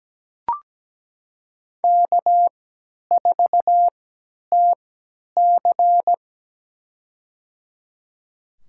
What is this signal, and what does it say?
Morse code, heard as a keyed tone, about 17 wpm, spaced as for 10 wpm, 700 Hz. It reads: K4TC
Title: USA Call Signs - 3x Character Spacing - 17wpm